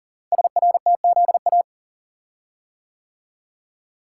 Morse code, heard as a keyed tone, about 42 wpm, 700 Hz. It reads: SRTZA
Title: Code Groups - 5 Letters - 40wpm